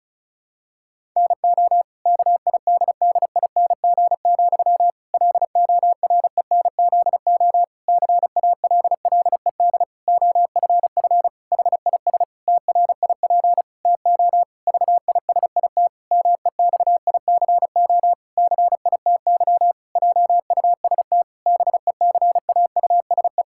Morse code, heard as a keyed tone, about 36 wpm, 700 Hz. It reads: NO KIDDING, LORENZO CALLED OFF HIS TRIP TO VISIT MEXICO CITY JUST BECAUSE
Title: Pangram Challenge v2 - 35wpm